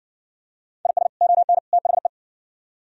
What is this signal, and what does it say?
Morse code, heard as a keyed tone, about 59 wpm, 700 Hz. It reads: IS ON THE